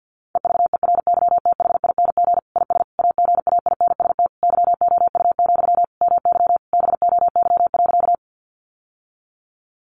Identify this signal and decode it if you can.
Morse code, heard as a keyed tone, about 48 wpm, 700 Hz. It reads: EVERYTHING IS AGAINST YOU, MY BOY.
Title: Sentences from Top 700 Words 50wpm